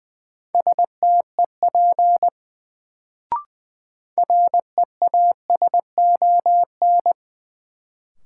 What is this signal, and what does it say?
Morse code, heard as a keyed tone, about 20 wpm, 700 Hz. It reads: STEP REASON
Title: Sets of 5 Words Encore - New Word Drill - 20wpm